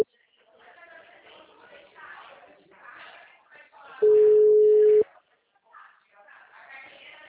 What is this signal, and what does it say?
Ringback tone (Europe, 425 Hz), with speech.